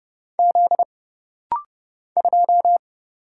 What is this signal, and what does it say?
Morse code, heard as a keyed tone, about 31 wpm, 700 Hz. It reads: Z 2
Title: Single Letter-Number Alt - Rapid-Fire 30wpm